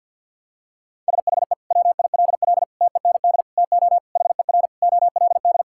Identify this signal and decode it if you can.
Morse code, heard as a keyed tone, about 52 wpm, 700 Hz. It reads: SHE WILL TEND TO HER OLD
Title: Sentences from Top 1000 Words - Review Part 5 - 50wpm